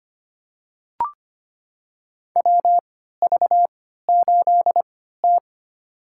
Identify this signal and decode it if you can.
Morse code, heard as a keyed tone, about 25 wpm, spaced as for 14 wpm, 700 Hz. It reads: WV8T
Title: USA Call Signs - 3x Character Spacing - 25wpm